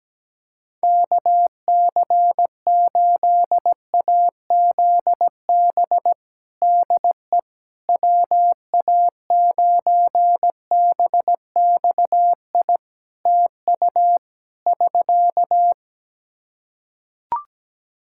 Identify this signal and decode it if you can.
Morse code, heard as a keyed tone, about 17 wpm, 700 Hz. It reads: KC8AZB DE WA9BXI TU <SK>
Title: K1USN SST QSO Practice 17wpm